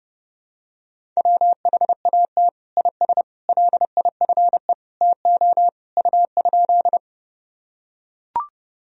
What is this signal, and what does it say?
Morse code, heard as a keyed tone, about 32 wpm, 700 Hz. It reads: WHAT IS LIFE TO U?
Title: Sentences from Top 300 Words and Abbreviations 30wpm